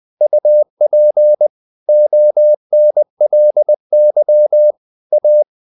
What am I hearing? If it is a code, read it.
Morse code, heard as a keyed tone, about 20 wpm, 600 Hz. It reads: UP ONLY A
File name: the old man and the sea / the old man and the sea part 1 20wpm